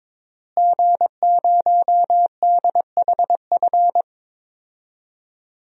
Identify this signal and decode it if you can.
Morse code, heard as a keyed tone, about 22 wpm, 700 Hz. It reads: G0DHF
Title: DXCC-100 Call Signs - 5 Characters - Repeated 2x - 22wpm